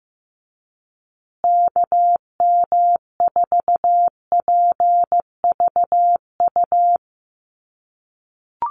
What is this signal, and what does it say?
Morse code, heard as a keyed tone, about 15 wpm, 700 Hz. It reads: KM4PVU
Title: USA Call Signs Repeated 3x 15wpm